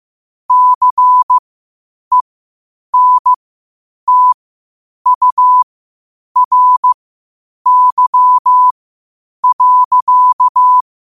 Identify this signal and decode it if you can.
Morse code, heard as a keyed tone, about 15 wpm, spaced as for 8 wpm, 1000 Hz. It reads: CENTURY.
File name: qotdfiles / 8wpm